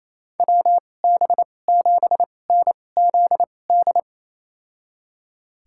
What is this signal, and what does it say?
Morse code, heard as a keyed tone, about 28 wpm, spaced as for 20 wpm, 700 Hz. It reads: WB7NZD
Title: USA Call Signs - 6 Characters - 2x Character Spacing - 28wpm